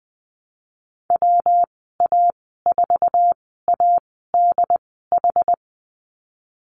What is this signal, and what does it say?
Morse code, heard as a keyed tone, about 20 wpm, spaced as for 15 wpm, 700 Hz. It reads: WA4ADH
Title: 140 Lesson 23g - Letter f - Call Signs 20wpm